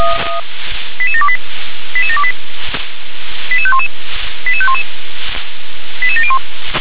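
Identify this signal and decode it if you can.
Five-tone selective call (ZVEI), heard as quick runs of five tones, tones of about 70 ms. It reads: 88728 88728 88511 88511 88701
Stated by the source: ZVEI five-tone selective call received on an SDR.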